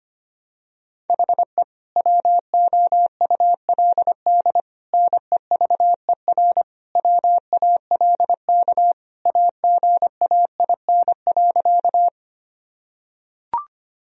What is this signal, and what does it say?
Morse code, heard as a keyed tone, about 25 wpm, 700 Hz. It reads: HE WOULD NEVER WALK AGAIN.
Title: Sentences from Top 1000 Words - Review Part 1 - 25wpm